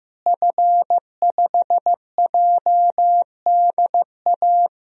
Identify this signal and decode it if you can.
Morse code, heard as a keyed tone, about 15 wpm, 700 Hz. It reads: F5JDA
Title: DXCC-100 Call Signs - 5 Characters - Repeated 3x - 15wpm